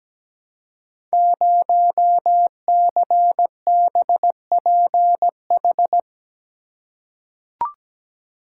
Morse code, heard as a keyed tone, about 17 wpm, 700 Hz. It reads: 0CBPH
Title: Code Groups - 5 Letter:Numbers - 17wpm